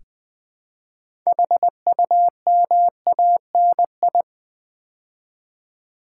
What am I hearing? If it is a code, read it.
Morse code, heard as a keyed tone, about 20 wpm, 700 Hz. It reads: HUMANI
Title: Build A Word - 3 Letters At A Time - 8-Letter Words - 20wpm